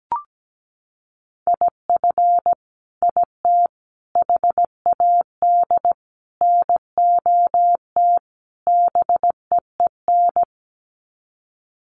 Morse code, heard as a keyed tone, about 17 wpm, 690 Hz. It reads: IF IT HAD NOT BEEN
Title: Sets of 5 Words Encore 17wpm